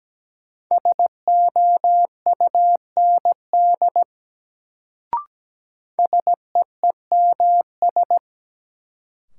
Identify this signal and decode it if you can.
Morse code, heard as a keyed tone, about 17 wpm, 700 Hz. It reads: SOUND SEEMS
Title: Sets of 3 Words - New Word Drill - 17wpm